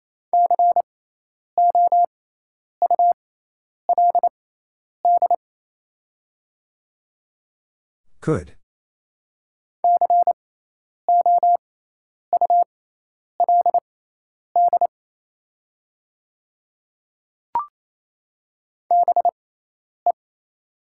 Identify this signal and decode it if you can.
Morse code, heard as a keyed tone, about 27 wpm, spaced as for 10 wpm, 690 Hz. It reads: COULD COULD BE
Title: 6x Character Spacing - Top 100 Words - 28wpm